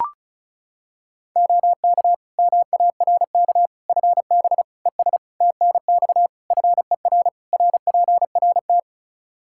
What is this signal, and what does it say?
Morse code, heard as a keyed tone, about 35 wpm, 700 Hz. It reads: OK MARK FB ES TNX FER RPRT
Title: Practice QSO - Exchange 2 - Info - 35wpm